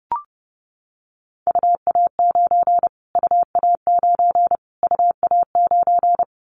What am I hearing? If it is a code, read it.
Morse code, heard as a keyed tone, about 30 wpm, 690 Hz. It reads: UA9 UA9 UA9
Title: Callsign Prefixes for Top 25 DXCC - Repeated 3x - 30wpm